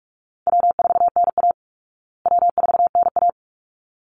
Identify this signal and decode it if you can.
Morse code, heard as a keyed tone, about 44 wpm, 700 Hz. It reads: W4NA W4NA
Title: Call Signs Repeated 3x - 45wpm